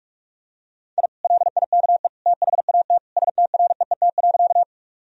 Morse code, heard as a keyed tone, about 47 wpm, 700 Hz. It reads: I LIKE THAT STREET.
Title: Sentences from Top 1000 Words - Review Part 3 - 45wpm